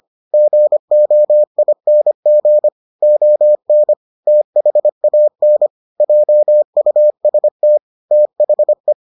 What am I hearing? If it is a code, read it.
Morse code, heard as a keyed tone, about 25 wpm, 600 Hz. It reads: GOING ON THAN JUST THE